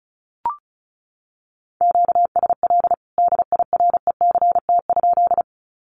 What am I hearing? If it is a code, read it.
Morse code, heard as a keyed tone, about 34 wpm, 700 Hz. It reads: QSL DIRECT?
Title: Common QSO Phrases 35wpm